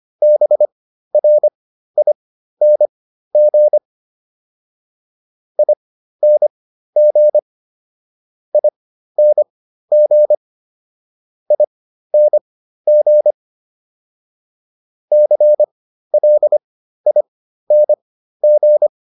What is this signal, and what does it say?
Morse code, heard as a keyed tone, about 25 wpm, spaced as for 13 wpm, 600 Hz. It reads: BRING ING ING ING CLING